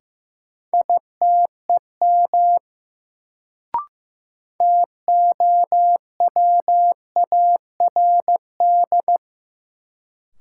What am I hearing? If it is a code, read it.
Morse code, heard as a keyed tone, about 15 wpm, 700 Hz. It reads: ITEM TOWARD
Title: Sentences from Top 1000 Words - Review Part 6 - New Word Drill - 15wpm